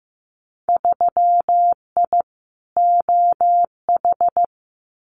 Morse code, heard as a keyed tone, about 15 wpm, 700 Hz. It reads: 3I OH
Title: Winter Field Day Exchanges Encore 15wpm